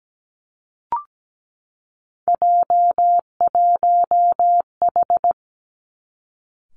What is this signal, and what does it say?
Morse code, heard as a keyed tone, about 17 wpm, 700 Hz. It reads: J1H
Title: Three Challenging Letter-Number Combinations 17wpm